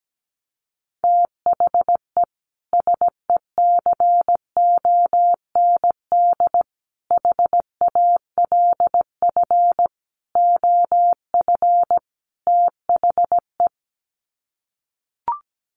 Morse code, heard as a keyed tone, about 17 wpm, 700 Hz. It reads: THE SECOND HALF OF THE
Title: Sets of 5 Words 17wpm